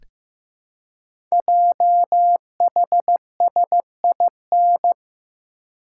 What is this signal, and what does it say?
Morse code, heard as a keyed tone, about 15 wpm, 700 Hz. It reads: JHSIN
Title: Code Groups - 5 Letters - 15wpm